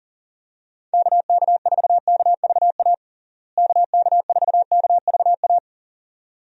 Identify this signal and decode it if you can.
Morse code, heard as a keyed tone, about 40 wpm, 700 Hz. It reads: KK4KVA KK4KVA
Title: USA Call Signs - 6 Characters - Repeated 2x - 40wpm